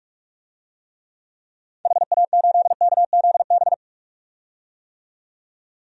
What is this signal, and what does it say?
Morse code, heard as a keyed tone, about 45 wpm, 700 Hz. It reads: HA8KZB
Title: DXCC-100 Call Signs - 6 Characters - 45wpm